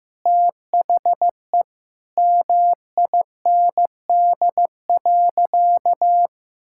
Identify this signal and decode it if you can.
Morse code, heard as a keyed tone, about 15 wpm, 700 Hz. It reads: THE MIND.
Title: Sentences from Top 1000 Words - Review Part 5 - 15wpm